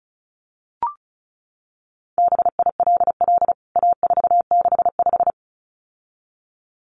Morse code, heard as a keyed tone, about 34 wpm, 700 Hz. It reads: BILL A465
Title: CWT Exchange 35wpm